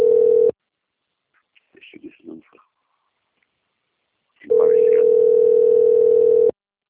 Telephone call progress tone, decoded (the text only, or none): ringback tone